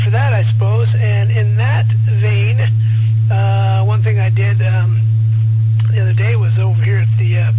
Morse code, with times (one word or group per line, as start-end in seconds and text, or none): none